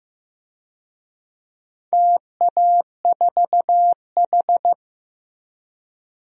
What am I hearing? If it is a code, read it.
Morse code, heard as a keyed tone, about 15 wpm, 700 Hz. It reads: TA4H
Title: DXCC-100 Call Signs - 4 Characters - Repeated 3x - 15wpm